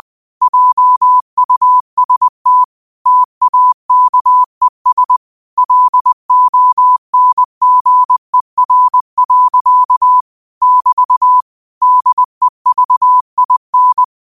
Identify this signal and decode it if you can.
Morse code, heard as a keyed tone, about 20 wpm, 1000 Hz. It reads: JUST TAKES LONGER. <BT> DEVIN
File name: qotdfiles / 20wpm